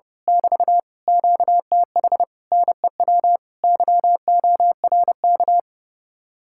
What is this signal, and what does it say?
Morse code, heard as a keyed tone, about 28 wpm, 700 Hz. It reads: <BT> QTH NEW YORK
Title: Speed-Racing Common QSO Phrases - 30wpm